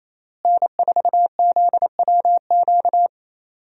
Morse code, heard as a keyed tone, about 28 wpm, 700 Hz. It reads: N4ZWQ
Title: USA Call Signs - 5 Characters - Repeated 3x - 28wpm